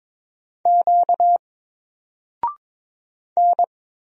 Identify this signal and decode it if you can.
Morse code, heard as a keyed tone, about 22 wpm, 700 Hz. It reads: Q N
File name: Single-Letter-Number-22wpm